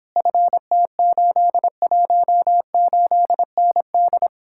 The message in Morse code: FT818ND

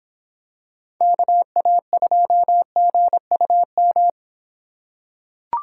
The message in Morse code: KA2GUM

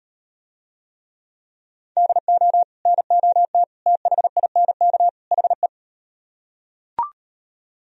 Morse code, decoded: DO NOT THINK HE

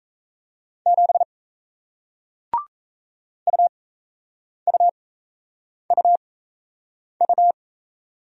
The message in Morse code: 7 UUUU

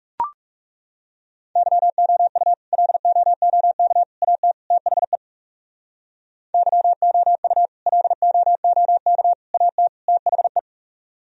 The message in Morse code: YOU LOOK AT THE YOU LOOK AT THE

